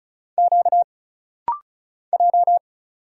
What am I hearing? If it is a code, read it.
Morse code, heard as a keyed tone, about 34 wpm, 700 Hz. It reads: Q J
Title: Single Letter-Number - Rapid-Fire 35wpm